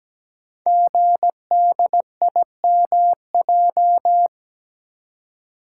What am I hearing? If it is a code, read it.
Morse code, heard as a keyed tone, about 17 wpm, 700 Hz. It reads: GDIMJ
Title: Code Groups - 5 Letters - 17wpm